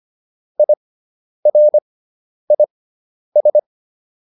Morse code, heard as a keyed tone, about 25 wpm, spaced as for 10 wpm, 600 Hz. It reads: IRIS